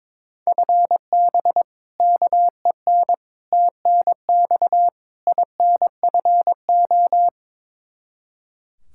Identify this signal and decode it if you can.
Morse code, heard as a keyed tone, about 22 wpm, 700 Hz. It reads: FB KEN TNX INFO
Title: Practice QSO - Exchange 3 - More Info - 22wpm